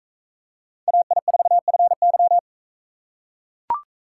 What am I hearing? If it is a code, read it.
Morse code, heard as a keyed tone, about 41 wpm, 700 Hz. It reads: AI4FY